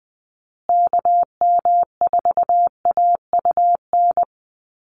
Morse code, heard as a keyed tone, about 20 wpm, 700 Hz. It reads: KM4AUN
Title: USA Call Signs Repeated 2x 20wpm